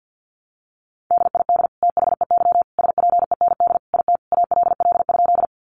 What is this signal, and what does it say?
Morse code, heard as a keyed tone, about 50 wpm, 700 Hz. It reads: DID THEY SPEND IT ALL?
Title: Sentences from Top 800 Words 50wpm